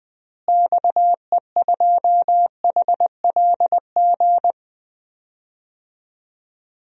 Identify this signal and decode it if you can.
Morse code, heard as a keyed tone, about 21 wpm, 700 Hz. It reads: XE2HLG
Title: DXCC-100 Call Signs - 6 Characters - Repeated 2x - 20wpm